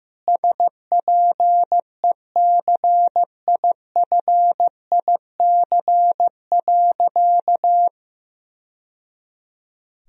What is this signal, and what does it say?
Morse code, heard as a keyed tone, about 15 wpm, 700 Hz. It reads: SPECIFIC.
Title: Sentences from Top 1000 Words - Review Part 3 - 15wpm